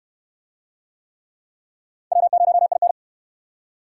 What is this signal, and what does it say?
Morse code, heard as a keyed tone, about 66 wpm, 700 Hz. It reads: W0IN